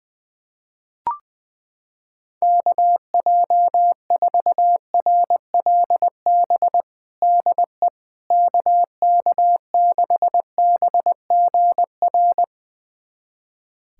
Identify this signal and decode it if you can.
Morse code, heard as a keyed tone, about 21 wpm, 700 Hz. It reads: KJ4RLB DE KK6BGR
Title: USA Callsign DE Callsign - 6 Characters - 20wpm